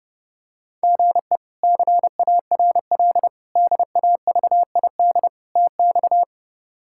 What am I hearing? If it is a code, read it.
Morse code, heard as a keyed tone, about 30 wpm, 700 Hz. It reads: GE CARL DAVID TX
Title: K1USN SST QSO Practice - 30wpm